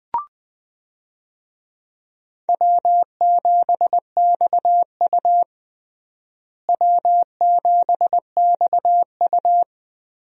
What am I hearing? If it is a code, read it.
Morse code, heard as a keyed tone, about 20 wpm, 700 Hz. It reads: W7XU W7XU
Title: USA Call Signs - 4 Characters - Repeated 2x - 20wpm